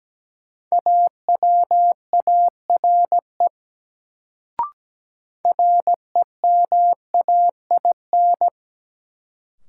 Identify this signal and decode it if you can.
Morse code, heard as a keyed tone, about 17 wpm, 700 Hz. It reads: AWARE REMAIN
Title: Sentences from Top 1200 Words - New Word Drill Alt - 17wpm